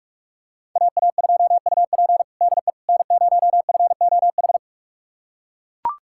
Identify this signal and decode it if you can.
Morse code, heard as a keyed tone, about 42 wpm, 700 Hz. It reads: AA2UP DE N0FOH
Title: USA Callsign DE Callsign - 5 Characters - 45wpm